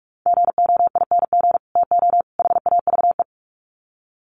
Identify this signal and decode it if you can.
Morse code, heard as a keyed tone, about 44 wpm, 700 Hz. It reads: GOING TO HAVE
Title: Sets of 3 Words 45wpm